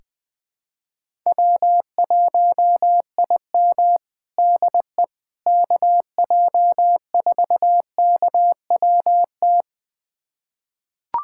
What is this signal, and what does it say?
Morse code, heard as a keyed tone, about 20 wpm, 700 Hz. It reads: W1IM DE KJ4KWT
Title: USA Callsign DE Callsign 20wpm